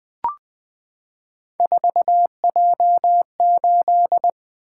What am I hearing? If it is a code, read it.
Morse code, heard as a keyed tone, about 20 wpm, 700 Hz. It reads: 4J8
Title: Three Letter-Number Combinations - 20wpm